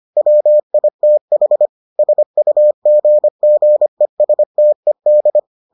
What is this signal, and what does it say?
Morse code, heard as a keyed tone, about 25 wpm, 600 Hz. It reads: WITH SUGGESTED